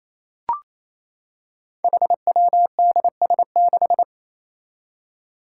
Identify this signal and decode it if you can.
Morse code, heard as a keyed tone, about 28 wpm, 700 Hz. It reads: HWDS6
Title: Five Challenging Letter-Number Combinations 28wpm